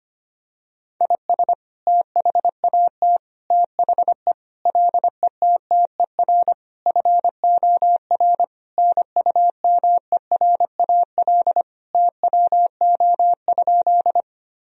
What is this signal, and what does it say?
Morse code, heard as a keyed tone, about 25 wpm, 700 Hz. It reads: IS THAT THE LETTER FOR NUMERAL TWO?
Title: Sentences from Top 1000 Words - Review Part 2 - 25wpm